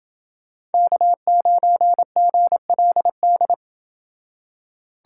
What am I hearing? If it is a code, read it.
Morse code, heard as a keyed tone, about 27 wpm, 700 Hz. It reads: K9GLD